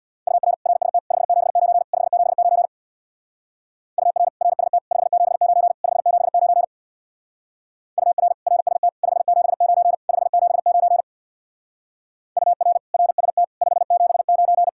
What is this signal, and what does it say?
Morse code, heard as a keyed tone, about 66 wpm, 700 Hz. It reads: UR RST 579 579 UR RST 579 579 UR RST 579 579 UR RST 579